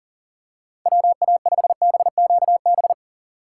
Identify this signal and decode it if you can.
Morse code, heard as a keyed tone, about 40 wpm, 700 Hz. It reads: WA5BQB
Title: USA Call Signs - 6 Characters - 6x Character Spacing - 40wpm